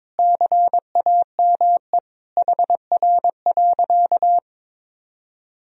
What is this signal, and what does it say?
Morse code, heard as a keyed tone, about 21 wpm, 700 Hz. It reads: CAME HR.
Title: Sentences from Top 500 Words and Abbreviations 22wpm